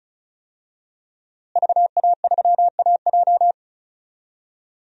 Morse code, heard as a keyed tone, about 36 wpm, 700 Hz. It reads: VA3AJ
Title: DXCC-100 Call Signs - 5 Characters - Repeated 3x - 35wpm